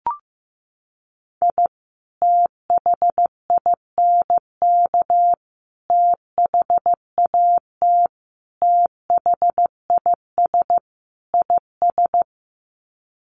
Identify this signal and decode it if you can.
Morse code, heard as a keyed tone, about 15 wpm, 700 Hz. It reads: I THINK THAT THIS IS
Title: Sets of 5 Words Encore2 15wpm